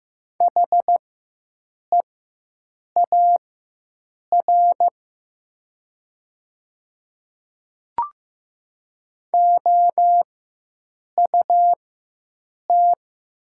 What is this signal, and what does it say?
Morse code, heard as a keyed tone, about 15 wpm, spaced as for 7 wpm, 700 Hz. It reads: HEAR OUT
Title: 4x Character Spacing - Top 300 Words - 15wpm